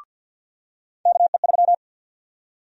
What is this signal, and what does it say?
Morse code, heard as a keyed tone, about 45 wpm, 700 Hz. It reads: KE3
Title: USA Call Sign Prefixes - 50wpm